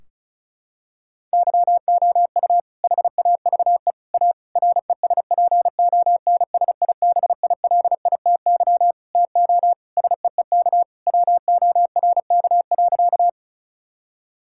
Morse code, heard as a keyed tone, about 36 wpm, 700 Hz. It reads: YOU HAVE A RESPONSIBILITY TO SEEK WORK.
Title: Sentences from Top 1600 Words 35wpm